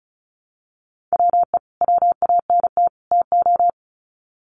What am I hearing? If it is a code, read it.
Morse code, heard as a keyed tone, about 34 wpm, 700 Hz. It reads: WE WANT TO
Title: Sets of 3 Words 35wpm